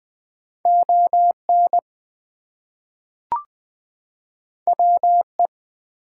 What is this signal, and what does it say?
Morse code, heard as a keyed tone, about 20 wpm, 700 Hz. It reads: ON WE
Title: Two Letter Words - 20wpm